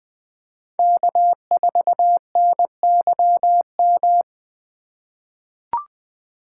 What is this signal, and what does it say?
Morse code, heard as a keyed tone, about 20 wpm, 700 Hz. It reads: K4NYM
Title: 2023 POTA Activator Call Signs - 20wpm